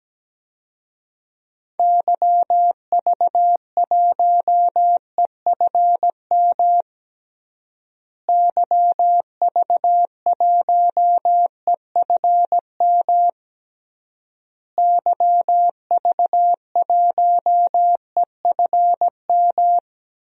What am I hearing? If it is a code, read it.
Morse code, heard as a keyed tone, about 17 wpm, 700 Hz. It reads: YV1EFM YV1EFM YV1EFM
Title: DXCC-100 Call Signs - 6 Characters - Repeated 3x - 17wpm